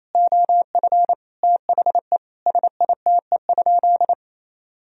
Morse code, heard as a keyed tone, about 27 wpm, 700 Hz. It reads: OF THE SITE?
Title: Sentences from Top 1000 Words - Review Part 4 - 28wpm